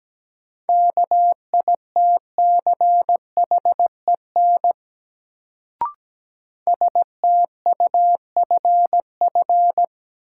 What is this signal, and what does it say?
Morse code, heard as a keyed tone, about 17 wpm, 700 Hz. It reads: KITCHEN STUFF
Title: Sentences from Top 1400 Words - New Word Drill - 17wpm